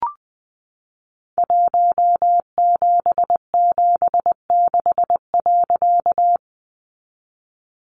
Morse code, heard as a keyed tone, about 20 wpm, 700 Hz. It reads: 1776.
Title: A Sound Of Thunder 20wpm